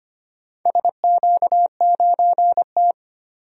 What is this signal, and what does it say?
Morse code, heard as a keyed tone, about 25 wpm, 700 Hz. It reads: SQ9T
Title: DXCC-100 Call Signs - 4 Characters - 25wpm